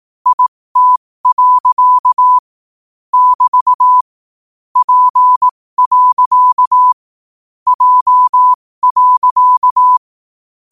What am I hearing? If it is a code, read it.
Morse code, heard as a keyed tone, about 18 wpm, spaced as for 15 wpm, 1000 Hz. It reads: IT. <BT> P. J.